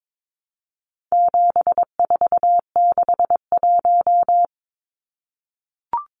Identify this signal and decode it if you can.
Morse code, heard as a keyed tone, about 22 wpm, 700 Hz. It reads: 7461
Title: Numbers - 4-Digit 22wpm